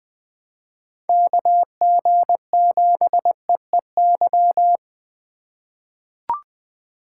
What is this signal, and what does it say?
Morse code, heard as a keyed tone, about 20 wpm, 700 Hz. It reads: KG7EEY
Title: USA Call Signs - 6 Characters - 3x Character Spacing - 20wpm